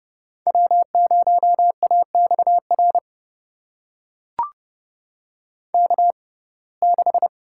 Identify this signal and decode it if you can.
Morse code, heard as a keyed tone, about 30 wpm, spaced as for 22 wpm, 700 Hz. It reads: W0AXR K6
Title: USA Call Signs - 6x Character Spacing - 30wpm